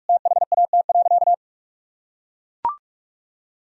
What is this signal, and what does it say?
Morse code, heard as a keyed tone, about 42 wpm, 700 Hz. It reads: THAT.
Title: Sentences from Top 200 Words and Abbreviations 45wpm